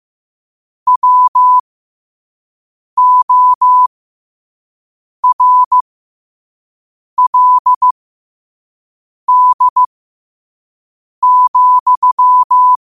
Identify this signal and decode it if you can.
Morse code, heard as a keyed tone, about 15 wpm, spaced as for 5 wpm, 1000 Hz. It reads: WORLD,
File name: qotdfiles / 5wpm